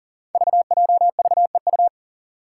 Morse code, heard as a keyed tone, about 37 wpm, 700 Hz. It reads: VJVEU
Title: Five Challenging Letter-Number Combinations 40wpm